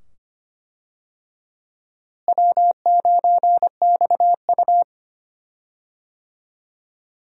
Morse code, heard as a keyed tone, about 25 wpm, 700 Hz. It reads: W9XU
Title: USA Call Signs - 4 Characters - Repeated 2x - 25wpm